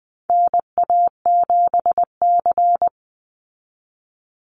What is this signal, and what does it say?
Morse code, heard as a keyed tone, about 20 wpm, 700 Hz. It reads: NA7C
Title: Call Signs Repeated 2x - 20wpm